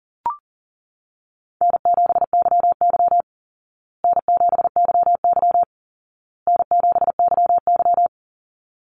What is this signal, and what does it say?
Morse code, heard as a keyed tone, about 40 wpm, 700 Hz. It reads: N7YY N7YY N7YY
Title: Easy Call Signs Repeated 3x - 40wpm